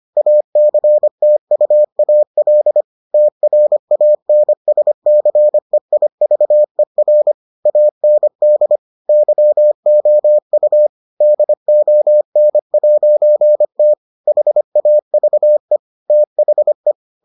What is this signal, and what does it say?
Morse code, heard as a keyed tone, about 25 wpm, 600 Hz. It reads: ACTUAL TRANSCEIVER AND YOU DON'T HAVE THE